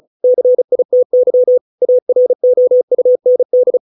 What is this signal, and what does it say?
Morse code, heard as a keyed tone, about 36 wpm, 500 Hz. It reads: CITY AROUND